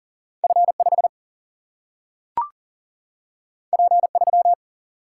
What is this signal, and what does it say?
Morse code, heard as a keyed tone, about 40 wpm, 700 Hz. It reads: F5 P3
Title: Single-Letter Single-Number 40wpm